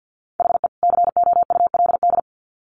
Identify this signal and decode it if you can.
Morse code, heard as a keyed tone, about 50 wpm, 700 Hz. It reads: HE COULD
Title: Sets of 2 Words 50wpm